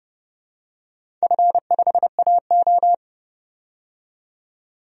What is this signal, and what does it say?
Morse code, heard as a keyed tone, about 30 wpm, 700 Hz. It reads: F5AO
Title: DXCC-100 Call Signs - 4 Characters - 30wpm